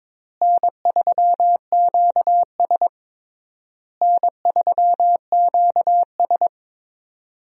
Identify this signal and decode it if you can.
Morse code, heard as a keyed tone, about 22 wpm, 700 Hz. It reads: N3QS N3QS
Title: USA Call Signs - 4 Characters - Repeated 3x - 22wpm